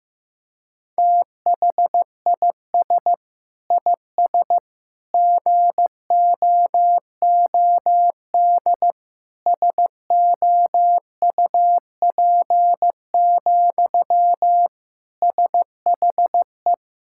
Morse code, heard as a keyed tone, about 15 wpm, 700 Hz. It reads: THIS IS GOOD SOUP, SHE
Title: Sentences from Top 1000 Words - Review Part 6 - 15wpm